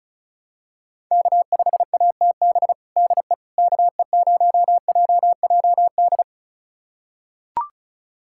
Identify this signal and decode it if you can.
Morse code, heard as a keyed tone, about 34 wpm, 700 Hz. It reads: K5ATB DE KE0JJD
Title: USA Callsign DE Callsign 35wpm